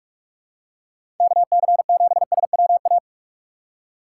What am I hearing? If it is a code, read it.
Morse code, heard as a keyed tone, about 43 wpm, 700 Hz. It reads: KC7SWA
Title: USA Call Signs - 6 Characters - 6x Character Spacing - 45wpm